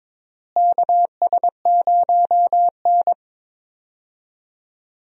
Morse code, heard as a keyed tone, about 22 wpm, 700 Hz. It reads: KS0N
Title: USA Call Signs - 4 Characters - Repeated 2x - 22wpm